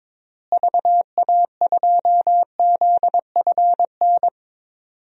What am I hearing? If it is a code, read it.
Morse code, heard as a keyed tone, about 22 wpm, 700 Hz. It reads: VA2ZFN
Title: DXCC-100 Call Signs - 6 Characters - Repeated 3x - 22wpm